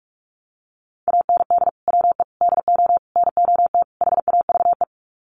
Morse code, heard as a keyed tone, about 44 wpm, 700 Hz. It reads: AND WE DO NOT HAVE
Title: Sets of 5 Words Encore 45wpm